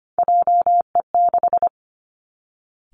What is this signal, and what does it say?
Morse code, heard as a keyed tone, about 25 wpm, 700 Hz. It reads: JE6
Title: Three Challenging Letter-Number Combinations 25wpm